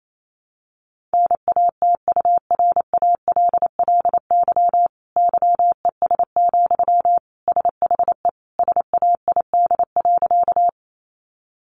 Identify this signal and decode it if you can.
Morse code, heard as a keyed tone, about 28 wpm, 700 Hz. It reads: NATURALLY YES, SHE SAID.
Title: Sentences from Top 1300 Words 28wpm